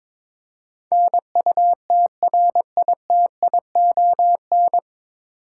Morse code, heard as a keyed tone, about 22 wpm, 700 Hz. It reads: NUTRITION
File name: 9 Letter Words - First 3 Letters Given - 22wpm